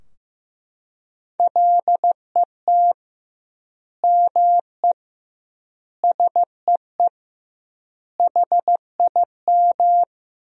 Morse code, heard as a keyed tone, about 15 wpm, 700 Hz. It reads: LET ME SEE HIM
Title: 2x Word-Spacing - Sentences from Top 300 Words - 15wpm